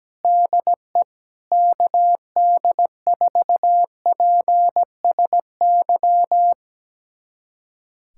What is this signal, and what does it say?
Morse code, heard as a keyed tone, about 17 wpm, 700 Hz. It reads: DE KD4PSY
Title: USA Callsign DE Callsign 17wpm